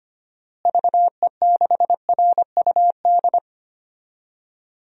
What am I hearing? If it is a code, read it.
Morse code, heard as a keyed tone, about 25 wpm, 700 Hz. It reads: VE6RUD
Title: DXCC-100 Call Signs - 6 Characters - Repeated 3x - 25wpm